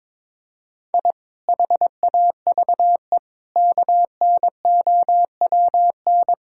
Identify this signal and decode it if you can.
Morse code, heard as a keyed tone, about 22 wpm, 700 Hz. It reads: I HAVE KNOWN
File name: practice / 22wpm / Sentences with 4-Word Phrases - 22wpm